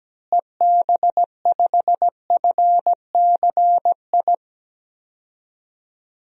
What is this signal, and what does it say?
Morse code, heard as a keyed tone, about 17 wpm, 700 Hz. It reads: EB5FCI
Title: DXCC-100 Call Signs - 6 Characters - Repeated 3x - 17wpm